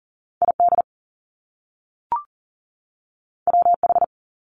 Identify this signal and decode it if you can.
Morse code, heard as a keyed tone, about 41 wpm, 700 Hz. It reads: ID WH
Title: Top Two Letter Combinations in Words 40wpm